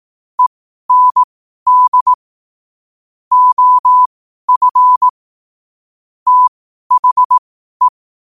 Morse code, heard as a keyed tone, about 18 wpm, spaced as for 12 wpm, 1000 Hz. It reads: END OF THE
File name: qotdfiles / 12wpm